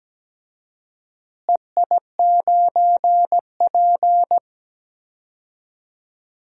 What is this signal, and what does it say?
Morse code, heard as a keyed tone, about 17 wpm, 700 Hz. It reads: EI9P
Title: DXCC-100 Call Signs - 4 Characters - Repeated 3x - 17wpm